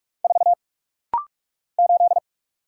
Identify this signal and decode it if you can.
Morse code, heard as a keyed tone, about 46 wpm, 700 Hz. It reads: 4 8
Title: Single Letter-Number - Rapid-Fire 45wpm